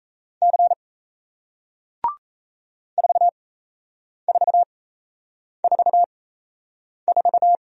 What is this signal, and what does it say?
Morse code, heard as a keyed tone, about 38 wpm, spaced as for 8 wpm, 700 Hz. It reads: C 4444